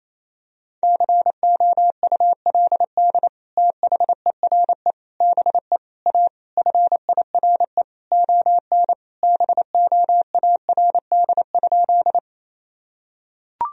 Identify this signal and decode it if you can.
Morse code, heard as a keyed tone, about 28 wpm, 700 Hz. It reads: COULD THERE BE A FIRE ON BOARD?
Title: Sentences from Top 1000 Words - Review Part 4 - 28wpm